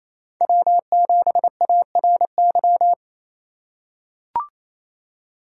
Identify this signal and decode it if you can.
Morse code, heard as a keyed tone, about 28 wpm, 700 Hz. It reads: W7ARY
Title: USA Call Signs - 5 Characters - 2x Character Spacing - 28wpm